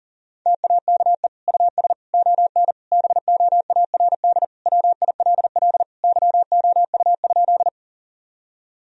Morse code, heard as a keyed tone, about 40 wpm, 700 Hz. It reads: TAKE US ON BOARD WILL YOU?
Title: Sentences from Top 1000 Words - Review Part 4 - 40wpm